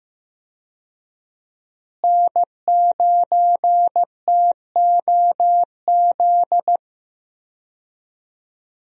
Morse code, heard as a keyed tone, about 15 wpm, 700 Hz. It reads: N9TOZ
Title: USA Call Signs - 5 Characters - Repeated 3x - 15wpm